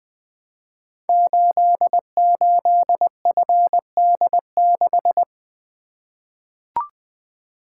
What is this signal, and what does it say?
Morse code, heard as a keyed tone, about 20 wpm, 700 Hz. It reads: 88FD6
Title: Code Groups - 5 Letter:Numbers - 20wpm